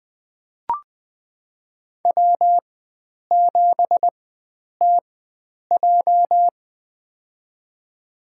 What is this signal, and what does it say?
Morse code, heard as a keyed tone, about 20 wpm, spaced as for 9 wpm, 700 Hz. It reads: W7TJ
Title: USA Call Signs - 4 Characters - 4x Character Spacing - 20wpm